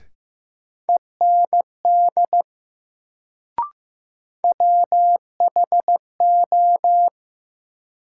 Morse code, heard as a keyed tone, about 15 wpm, 700 Hz. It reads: END WHO
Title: Sets of 2 Words - New Word Drill - 15wpm